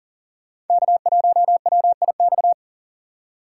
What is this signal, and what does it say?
Morse code, heard as a keyed tone, about 52 wpm, 700 Hz. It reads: K1WIX